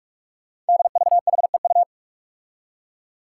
Morse code, heard as a keyed tone, about 44 wpm, 700 Hz. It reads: DVHEU
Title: Five Challenging Letter-Number Combinations 45wpm